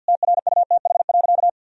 Morse code, heard as a keyed tone, about 50 wpm, 700 Hz. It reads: TRUTH.
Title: Sentences from Top 1000 Words - Review Part 2 - 50wpm